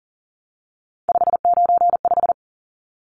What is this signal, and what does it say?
Morse code, heard as a keyed tone, about 40 wpm, 700 Hz. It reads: 595
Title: Numbers - 3-Digit 40wpm